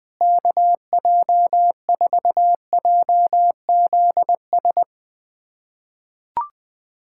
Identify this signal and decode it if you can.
Morse code, heard as a keyed tone, about 20 wpm, 700 Hz. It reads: KJ4JZS
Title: USA Callsign DE Callsign - 6 Characters - 20wpm